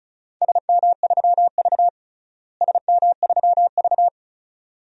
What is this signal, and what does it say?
Morse code, heard as a keyed tone, about 36 wpm, 700 Hz. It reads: SM3V SM3V
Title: DXCC-100 Call Signs - 4 Characters - Repeated 2x - 35wpm